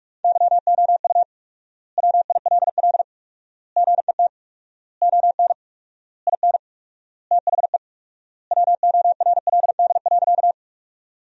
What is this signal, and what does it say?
Morse code, heard as a keyed tone, about 46 wpm, 700 Hz. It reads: YOU WILL GET ON IN THE WORLD.
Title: 4x Word-Spacing - Sentences from Top 200 Words - 45wpm